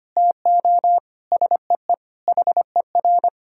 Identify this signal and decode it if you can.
Morse code, heard as a keyed tone, about 25 wpm, 700 Hz. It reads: TO SEE HER
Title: Sentences from Top 1000 Words - Review Part 4 - 25wpm